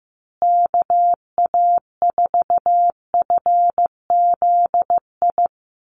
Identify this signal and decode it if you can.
Morse code, heard as a keyed tone, about 15 wpm, 700 Hz. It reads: KA4FZI
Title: USA Call Signs Repeated 2x 15wpm